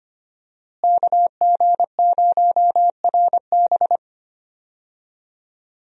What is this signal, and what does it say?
Morse code, heard as a keyed tone, about 25 wpm, 700 Hz. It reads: KG0RB
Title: USA Call Signs - 5 Characters - Repeated 3x - 25wpm